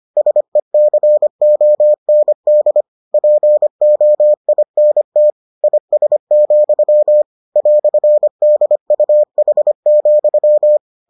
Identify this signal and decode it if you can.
Morse code, heard as a keyed tone, about 25 wpm, 600 Hz. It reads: SECOND POINT IS, "DUH,